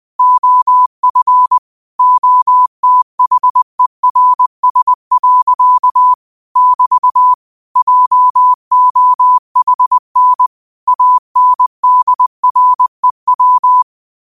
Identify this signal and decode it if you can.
Morse code, heard as a keyed tone, about 20 wpm, 1000 Hz. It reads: OF OTHERS. <BT> JOHN ANDREW
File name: qotdfiles / 20wpm